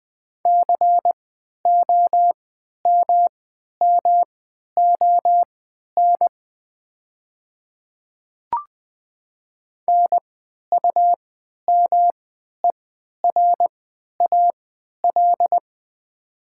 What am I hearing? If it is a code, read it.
Morse code, heard as a keyed tone, about 20 wpm, spaced as for 11 wpm, 700 Hz. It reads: COMMON NUMERAL
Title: 3x Character Spacing - Top 500 Words - 20wpm